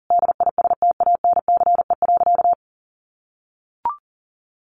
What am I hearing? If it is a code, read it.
Morse code, heard as a keyed tone, about 40 wpm, 700 Hz. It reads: DISTANCE.
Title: Sentences from Top 1200 Words 40wpm